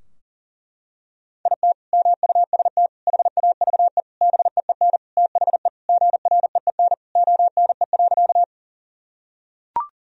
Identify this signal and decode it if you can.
Morse code, heard as a keyed tone, about 37 wpm, 700 Hz. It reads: IT MUST HAVE BEEN THE GREEN ONE.
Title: Sentences from Top 1000 Words - Review Part 2 - 40wpm